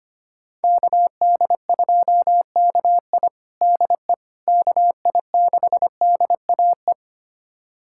Morse code, heard as a keyed tone, about 25 wpm, 700 Hz. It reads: KD2KI DE KI6DAE
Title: USA Callsign DE Callsign 25wpm